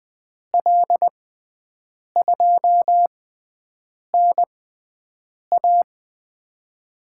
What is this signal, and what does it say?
Morse code, heard as a keyed tone, about 20 wpm, spaced as for 7 wpm, 700 Hz. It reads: L2NA
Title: USA Call Signs - 5 Characters - 6x Character Spacing - 20wpm